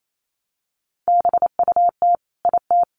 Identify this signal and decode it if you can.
Morse code, heard as a keyed tone, about 28 wpm, 700 Hz. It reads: BUT IT
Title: Sentences from Top 500 Words 28wpm